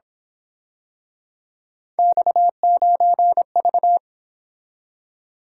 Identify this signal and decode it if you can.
Morse code, heard as a keyed tone, about 26 wpm, 700 Hz. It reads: X9V
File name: Speed-Racing - Three Letter-Number Combinations - 22wpm